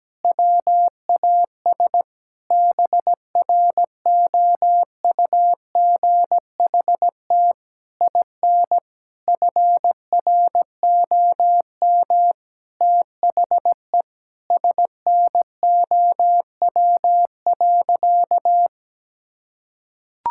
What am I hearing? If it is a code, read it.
Morse code, heard as a keyed tone, about 17 wpm, 700 Hz. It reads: WAS BROUGHT IN FROM THE SNOW.
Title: Sentences from Top 1000 Words - Review Part 5 - 17wpm